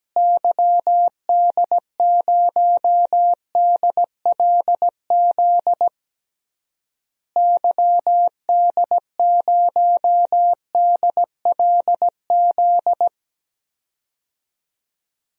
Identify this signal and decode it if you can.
Morse code, heard as a keyed tone, about 17 wpm, 700 Hz. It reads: YD0DLZ YD0DLZ
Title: DXCC-100 Call Signs - 6 Characters - Repeated 3x - 17wpm